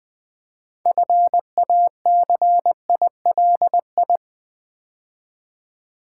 Morse code, heard as a keyed tone, about 20 wpm, 700 Hz. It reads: FACILI
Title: Build A Word - 2 Letters At A Time - 8-Letter Words - 20wpm